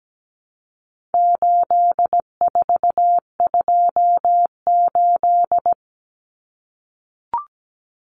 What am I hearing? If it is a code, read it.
Morse code, heard as a keyed tone, about 17 wpm, 700 Hz. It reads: 8428
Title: Numbers - 4-Digit 17wpm